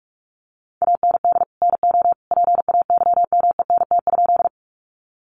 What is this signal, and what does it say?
Morse code, heard as a keyed tone, about 44 wpm, 700 Hz. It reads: AND NO PAYMENT?
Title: Sentences from Top 1200 Words 45wpm